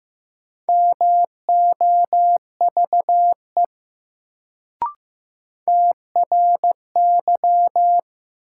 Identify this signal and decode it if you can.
Morse code, heard as a keyed tone, about 15 wpm, 700 Hz. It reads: MOVE TRY
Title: Sentences from Top 200 Words - New Word Drill - 15wpm